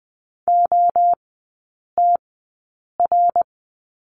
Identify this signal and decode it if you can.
Morse code, heard as a keyed tone, about 20 wpm, spaced as for 8 wpm, 700 Hz. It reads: OTR
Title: 093 Lesson 16e - Letter c - Three Characters 20wpm